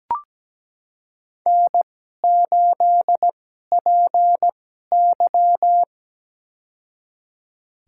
Morse code, heard as a keyed tone, about 17 wpm, spaced as for 12 wpm, 700 Hz. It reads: N8PY
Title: USA Call Signs - 4 Characters - 2x Character Spacing - 17wpm